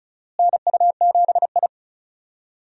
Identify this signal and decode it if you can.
Morse code, heard as a keyed tone, about 36 wpm, 700 Hz. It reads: NU7I
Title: USA Call Signs - 4 Characters - Repeated 2x - 35wpm